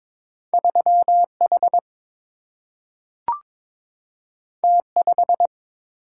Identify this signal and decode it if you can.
Morse code, heard as a keyed tone, about 22 wpm, 700 Hz. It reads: 3 H T 5
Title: Single-Letter Single-Number 22wpm